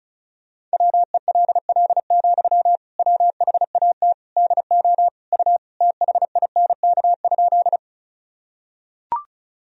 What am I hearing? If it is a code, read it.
Morse code, heard as a keyed tone, about 34 wpm, 690 Hz. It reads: WELL, WHAT DO U THINK?
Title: Sentences from Top 200 Words and Abbreviations 35wpm